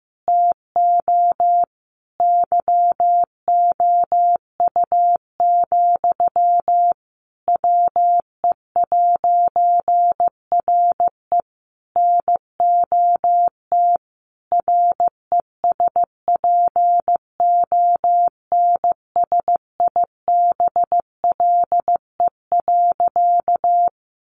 Morse code, heard as a keyed tone, about 15 wpm, 700 Hz. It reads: TO YOU, WE'RE NOT RESPONSIBLE.
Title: A Sound Of Thunder 15wpm